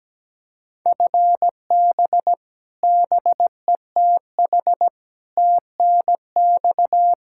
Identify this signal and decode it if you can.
Morse code, heard as a keyed tone, about 17 wpm, 700 Hz. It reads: FB BETH TNX
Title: Practice QSO - Exchange 3 - More Info - 17wpm